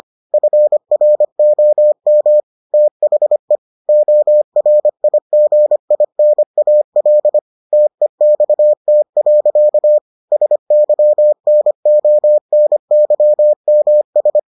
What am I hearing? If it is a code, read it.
Morse code, heard as a keyed tone, about 25 wpm, 600 Hz. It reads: FROM THE ORIGINAL TEXT. SYNONYMS